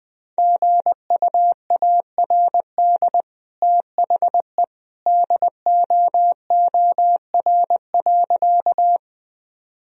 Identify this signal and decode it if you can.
Morse code, heard as a keyed tone, about 20 wpm, 700 Hz. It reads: GUARD THE DOOR.
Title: Sentences from Top 1000 Words - Review Part 4 - 20wpm